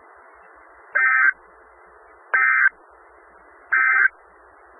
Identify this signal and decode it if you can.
Emergency Alert System (SAME) tones, heard as harsh buzzing bursts. It is heard from 0.8 to 4.1 s.
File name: Skywarn/2013/2013-03-18-Newton County WA4ASI Repeater